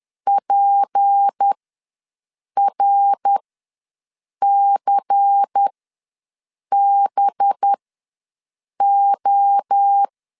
Morse code, heard as a keyed tone, about 10 wpm, spaced as for 6 wpm, 800 Hz. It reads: PRCBO
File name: LEZIONI / 6 PKX0 / 6 P ws 10wpm 3751